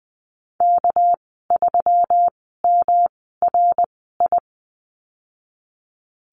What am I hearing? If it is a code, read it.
Morse code, heard as a keyed tone, about 20 wpm, spaced as for 14 wpm, 700 Hz. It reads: K3MRI
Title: 219 Lesson 35g - Number 0 - Call Signs 20wpm